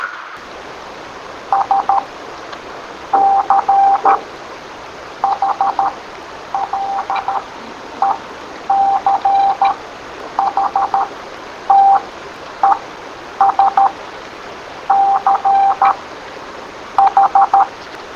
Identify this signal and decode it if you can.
Morse code, heard as a keyed tone, about 13 wpm, spaced as for 7 wpm, 780 Hz. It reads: SCHLECHTE SCH